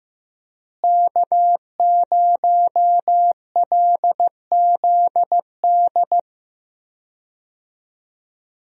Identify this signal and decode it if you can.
Morse code, heard as a keyed tone, about 15 wpm, 700 Hz. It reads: K0LZD
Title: USA Call Signs - 5 Characters - Repeated 3x - 15wpm